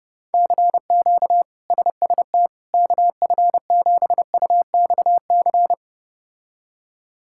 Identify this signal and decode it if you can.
Morse code, heard as a keyed tone, about 31 wpm, 700 Hz. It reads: CQ SST KF7UXC
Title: K1USN SST QSO Practice - 30wpm